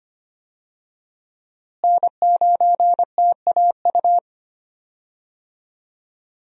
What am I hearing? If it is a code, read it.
Morse code, heard as a keyed tone, about 25 wpm, 700 Hz. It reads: N9TAU
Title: USA Call Signs - 5 Characters - Repeated 2x - 25wpm